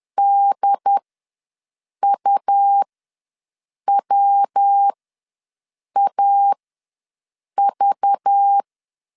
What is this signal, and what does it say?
Morse code, heard as a keyed tone, about 10 wpm, spaced as for 6 wpm, 800 Hz. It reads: DUWAV
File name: LEZIONI / 10 SIGNS / 10D COMA ws 10wpm 4314